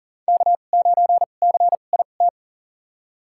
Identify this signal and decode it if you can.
Morse code, heard as a keyed tone, about 40 wpm, spaced as for 29 wpm, 700 Hz. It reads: K9CIT
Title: USA Call Signs - 5 Characters - 2x Character Spacing - 40wpm